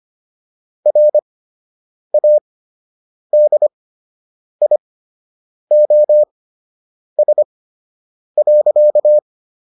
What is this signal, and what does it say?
Morse code, heard as a keyed tone, about 25 wpm, spaced as for 8 wpm, 600 Hz. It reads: RADIOS.